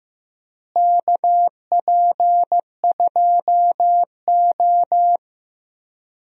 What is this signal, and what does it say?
Morse code, heard as a keyed tone, about 15 wpm, 700 Hz. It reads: KP2O
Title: DXCC-100 Call Signs - 4 Characters - Repeated 3x - 15wpm